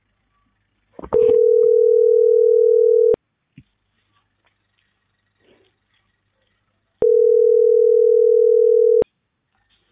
Ringback tone (North America), 440 + 480 Hz.